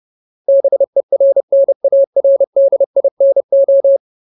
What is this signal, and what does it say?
Morse code, heard as a keyed tone, about 30 wpm, 550 Hz. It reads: BERNARDINO
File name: Ultimate Word List / 30WPM Ultimate